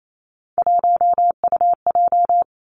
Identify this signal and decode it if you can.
Morse code, heard as a keyed tone, about 28 wpm, 700 Hz. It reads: 1UJ
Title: Three Challenging Letter-Number Combinations 28wpm